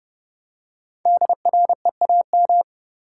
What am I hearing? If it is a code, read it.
Morse code, heard as a keyed tone, about 30 wpm, 700 Hz. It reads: DREAM
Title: Five Letter Words - 30wpm